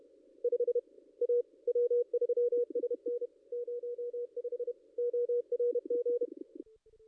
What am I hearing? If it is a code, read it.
Morse code, heard as a keyed tone, about 30 wpm, 440 Hz. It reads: 5 A W3SN 05 ORG
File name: contests / ss-cw-k5tr-op-n5rz -2200